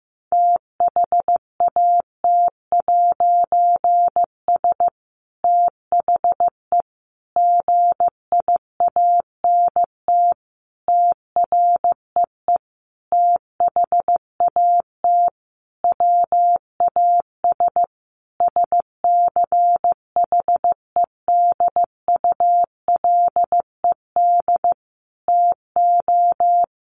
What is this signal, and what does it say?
Morse code, heard as a keyed tone, about 15 wpm, 700 Hz. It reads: THAT'S THE GIANT TREE THAT WAS SCHEDULED TO
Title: A Sound Of Thunder 15wpm